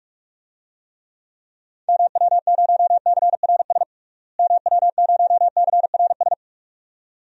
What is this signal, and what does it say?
Morse code, heard as a keyed tone, about 46 wpm, 700 Hz. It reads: MW0CRS MW0CRS
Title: DXCC-100 Call Signs - 6 Characters - Repeated 2x - 45wpm